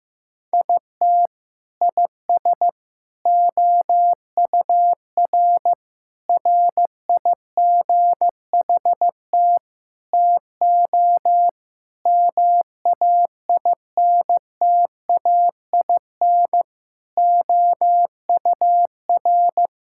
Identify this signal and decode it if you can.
Morse code, heard as a keyed tone, about 16 wpm, 700 Hz. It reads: IT IS OUR RIGHT TO MAINTAIN OUR
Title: Sentences from Top 1000 Words - Review Part 5 - 15wpm